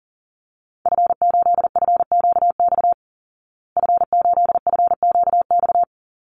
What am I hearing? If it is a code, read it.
Morse code, heard as a keyed tone, about 40 wpm, 700 Hz. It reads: F8FQX F8FQX
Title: Call Signs Repeated 3x - 40wpm